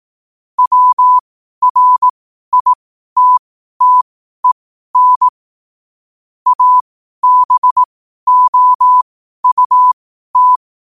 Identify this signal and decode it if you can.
Morse code, heard as a keyed tone, about 18 wpm, spaced as for 12 wpm, 1000 Hz. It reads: WRITTEN ABOUT
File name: qotdfiles / 12wpm